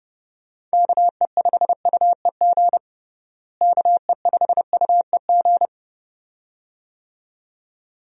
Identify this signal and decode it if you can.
Morse code, heard as a keyed tone, about 32 wpm, 700 Hz. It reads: KE5UEG KE5UEG
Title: USA Call Signs - 6 Characters - Repeated 2x - 30wpm